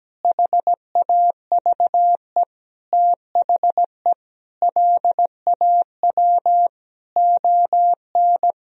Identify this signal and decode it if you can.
Morse code, heard as a keyed tone, about 17 wpm, 700 Hz. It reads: HAVE THE LAW ON
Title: Sentences from Top 1000 Words - Review Part 3 - 17wpm